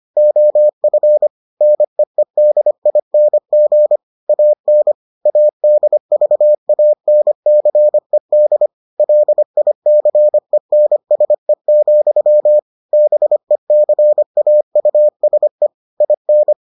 Morse code, heard as a keyed tone, about 25 wpm, 600 Hz. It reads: OF NEEDING AN ADVANCED LICENSE, BECAUSE IN